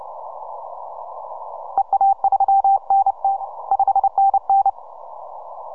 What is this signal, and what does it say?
Morse code, heard as a keyed tone, about 30 wpm, 780 Hz. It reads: EA3NT 5NN
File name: simple-file-list / H40GC 15MCW